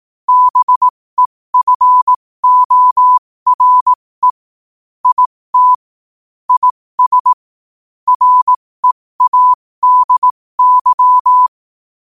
Morse code, heard as a keyed tone, about 18 wpm, spaced as for 15 wpm, 1000 Hz. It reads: BEFORE IT IS READY
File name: qotdfiles / 15wpm